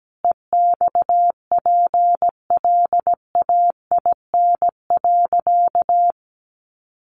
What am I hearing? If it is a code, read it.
Morse code, heard as a keyed tone, about 17 wpm, 700 Hz. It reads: EXPLAIN.
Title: Sentences from Top 1100 Words 17wpm